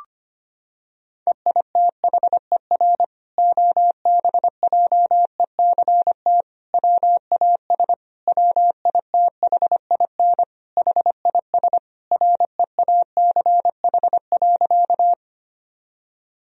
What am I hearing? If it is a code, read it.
Morse code, heard as a keyed tone, about 25 wpm, 700 Hz. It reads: EITHER OBJECT WAS WITHIN HIS REACH.
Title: Sentences from Top 1000 Words - Review Part 4 - 25wpm